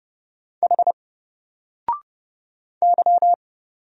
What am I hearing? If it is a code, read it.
Morse code, heard as a keyed tone, about 30 wpm, 700 Hz. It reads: H Y
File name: Single-Letter-Number-30wpm